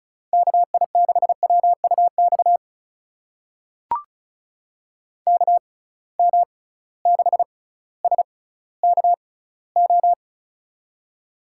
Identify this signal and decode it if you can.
Morse code, heard as a keyed tone, about 36 wpm, spaced as for 18 wpm, 700 Hz. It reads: KI6WUX KM6SKO
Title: USA Call Signs - 6x Character Spacing - 35wpm